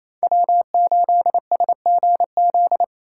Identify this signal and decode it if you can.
Morse code, heard as a keyed tone, about 28 wpm, 700 Hz. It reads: W8SGZ
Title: USA Call Signs - 5 Characters - 4x Character Spacing - 28wpm